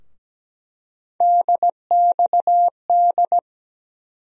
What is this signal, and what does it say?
Morse code, heard as a keyed tone, about 17 wpm, 700 Hz. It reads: DXD
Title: USA Call Sign Postfixes - 3 Letters 17wpm